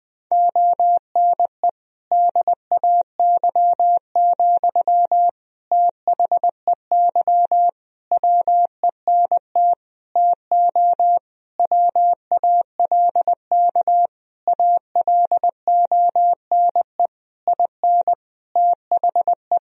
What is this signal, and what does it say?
Morse code, heard as a keyed tone, about 20 wpm, 700 Hz. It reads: ONE DAY, THEY WENT TO WALK ALONE IN THE
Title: Sentences from Top 1000 Words - Review Part 2 - 20wpm